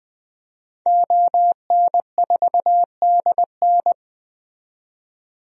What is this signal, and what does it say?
Morse code, heard as a keyed tone, about 20 wpm, 700 Hz. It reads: ON4DN
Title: DXCC-100 Call Signs - 5 Characters - 20wpm